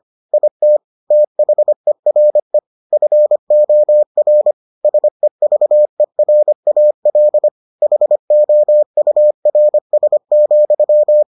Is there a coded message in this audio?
Yes: Morse code from 0.3 to 11.2 s, heard as a keyed tone, about 25 wpm, 600 Hz. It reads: IT THERE FOR SEVERAL HOURS,